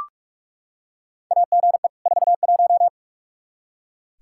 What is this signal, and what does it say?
Morse code, heard as a keyed tone, about 42 wpm, 700 Hz. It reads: AGE 41
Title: Practice QSO - Exchange 3 - More Info - 45wpm